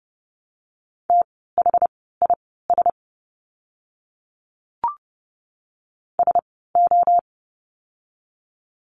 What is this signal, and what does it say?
Morse code, heard as a keyed tone, about 30 wpm, spaced as for 17 wpm, 700 Hz. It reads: THIS SO
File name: course / 30:10wpm / 230 Lesson 37f - Letter x - Words 30wpm